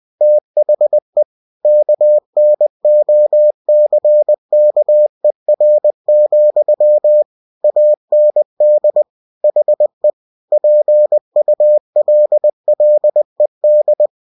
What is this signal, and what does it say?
Morse code, heard as a keyed tone, about 20 wpm, 600 Hz. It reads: THE KNOCKER, AND HE PULLED
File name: cw-books / winnie-the-pooh / winnie-20wpm-page-0029